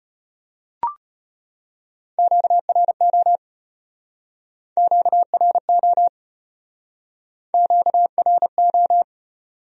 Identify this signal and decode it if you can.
Morse code, heard as a keyed tone, about 35 wpm, 700 Hz. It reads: QRO QRO QRO